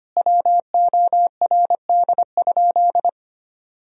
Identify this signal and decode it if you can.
Morse code, heard as a keyed tone, about 25 wpm, 700 Hz. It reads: WORD?
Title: Sentences from Top 1000 Words - Review Part 4 - 25wpm